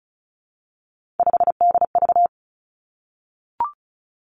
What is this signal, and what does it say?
Morse code, heard as a keyed tone, about 36 wpm, 700 Hz. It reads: 5DV
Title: Three Challenging Letter-Number Combinations 35wpm